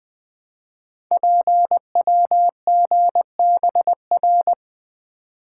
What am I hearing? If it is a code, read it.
Morse code, heard as a keyed tone, about 20 wpm, 700 Hz. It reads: PWGBR
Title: Code Groups - 5 Letters - 20wpm